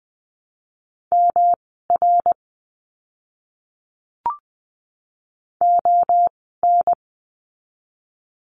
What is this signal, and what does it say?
Morse code, heard as a keyed tone, about 19 wpm, spaced as for 14 wpm, 700 Hz. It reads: MR ON
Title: 244 Lesson 39f - Letter z - Words 20wpm